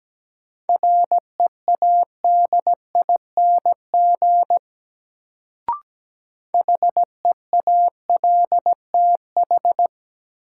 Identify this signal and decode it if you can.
Morse code, heard as a keyed tone, about 17 wpm, 700 Hz. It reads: READING HEALTH
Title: Sentences from Top 600 Words - New Word Drill - 17wpm